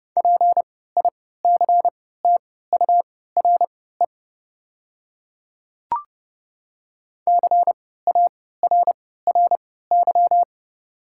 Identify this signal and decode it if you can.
Morse code, heard as a keyed tone, about 30 wpm, spaced as for 17 wpm, 700 Hz. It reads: PICTURE CARRY
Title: 3x Character Spacing - Top 300 Words - 30wpm